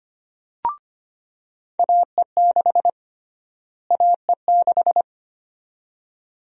Morse code, heard as a keyed tone, about 25 wpm, 700 Hz. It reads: AE6 AE6
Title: USA Call Sign Prefixes - Repeated 2x - 25wpm